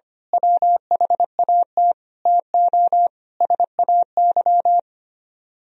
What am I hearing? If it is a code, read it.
Morse code, heard as a keyed tone, about 25 wpm, 700 Hz. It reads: WHAT TO SAY
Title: Speed-Racing - Sets of 5 Words 25wpm